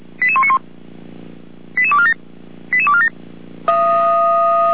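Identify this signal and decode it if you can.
Five-tone selective call (ZVEI), heard as quick runs of five tones, tones of about 70 ms. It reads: 88191 88247 88247
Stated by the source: ZVEI five-tone selective call received on an SDR.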